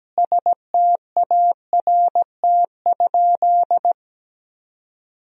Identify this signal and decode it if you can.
Morse code, heard as a keyed tone, about 17 wpm, 700 Hz. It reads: START?
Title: Sentences from Top 1000 Words - Review Part 1 - 17wpm